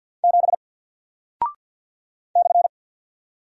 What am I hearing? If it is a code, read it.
Morse code, heard as a keyed tone, about 48 wpm, 700 Hz. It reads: 7 /
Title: Single Character (Letter:Number:Punctuation:Prosign) Alt 50wpm